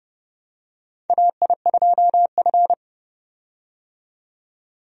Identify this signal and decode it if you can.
Morse code, heard as a keyed tone, about 32 wpm, 700 Hz. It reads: AI2F
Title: USA Call Signs - 4 Characters - 30wpm